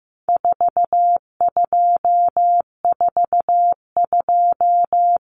Morse code, heard as a keyed tone, about 15 wpm, 700 Hz. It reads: 4242
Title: Numbers - 4-Digit 15wpm